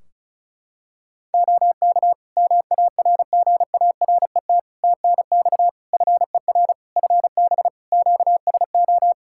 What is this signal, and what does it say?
Morse code, heard as a keyed tone, about 34 wpm, 700 Hz. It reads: OK MARGARET TNX FER FB QSO
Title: Practice QSO - Exchange 4 - Ending - 35wpm